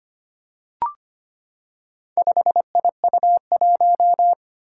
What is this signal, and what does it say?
Morse code, heard as a keyed tone, about 25 wpm, 700 Hz. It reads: 5IU1
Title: Four Challenging Letter-Number Combinations 25wpm